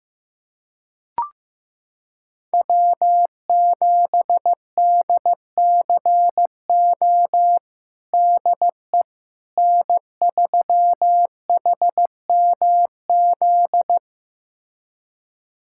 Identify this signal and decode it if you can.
Morse code, heard as a keyed tone, about 15 wpm, 700 Hz. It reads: W7DCO DE N3HMZ
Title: USA Callsign DE Callsign - 5 Characters - 15wpm